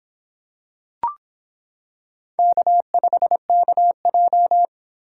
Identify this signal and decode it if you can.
Morse code, heard as a keyed tone, about 26 wpm, 700 Hz. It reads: K5KJ